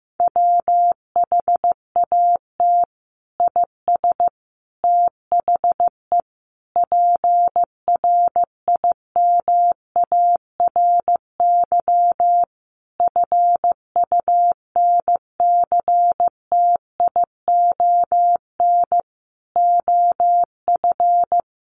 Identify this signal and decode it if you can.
Morse code, heard as a keyed tone, about 15 wpm, 700 Hz. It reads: WHAT IS THE PRIMARY FUNCTION OF
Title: Sentences from Top 1100 Words 15wpm